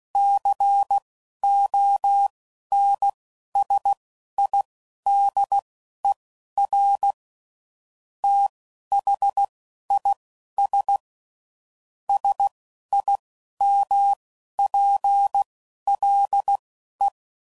Morse code, heard as a keyed tone, about 16 wpm, spaced as for 11 wpm, 800 Hz. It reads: CONSIDER THIS SIMPLE
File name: CW Code Groups @10WPM / Technical Subjects / OR Gate